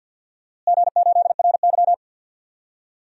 Morse code, heard as a keyed tone, about 51 wpm, 700 Hz. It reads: G8RY
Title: DXCC-100 Call Signs - 4 Characters - Repeated 2x - 50wpm